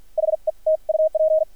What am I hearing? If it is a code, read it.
Morse code, heard as a keyed tone, about 53 wpm, spaced as for 45 wpm, 630 Hz. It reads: HE T UP
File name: archived / CRSnet